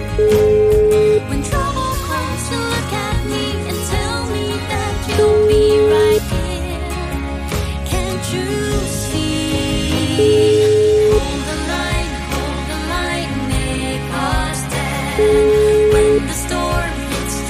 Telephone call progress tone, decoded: ringback tone

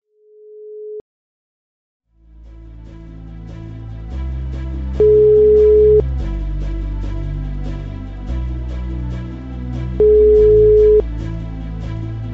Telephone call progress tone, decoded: ringback tone

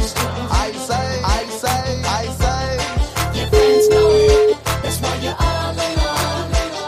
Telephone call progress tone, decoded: ringback tone

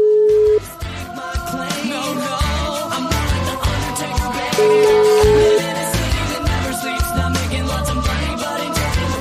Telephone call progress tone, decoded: ringback tone

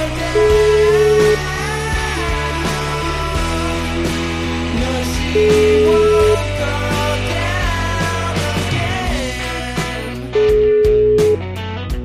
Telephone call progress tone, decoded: ringback tone

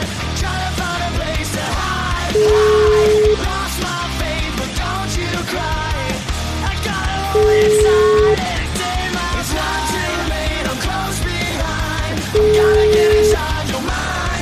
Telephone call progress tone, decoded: ringback tone

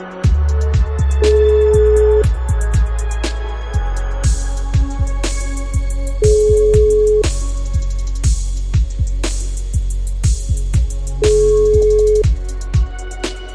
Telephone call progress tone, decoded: ringback tone

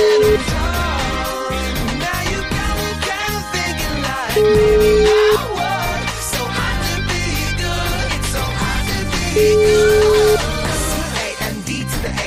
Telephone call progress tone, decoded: ringback tone

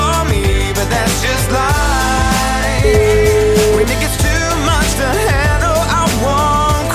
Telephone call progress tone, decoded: ringback tone

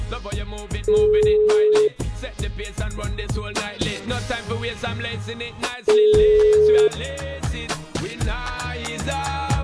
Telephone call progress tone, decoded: ringback tone

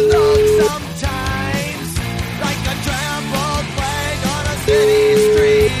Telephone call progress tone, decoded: ringback tone

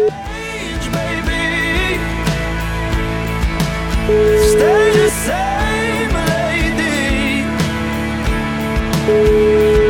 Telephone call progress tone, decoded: ringback tone

